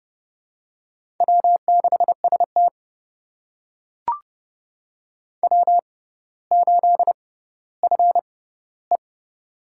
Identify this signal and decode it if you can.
Morse code, heard as a keyed tone, about 30 wpm, spaced as for 15 wpm, 700 Hz. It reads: W6ST W8FE
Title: USA Call Signs - 4 Characters - 6x Character Spacing - 30wpm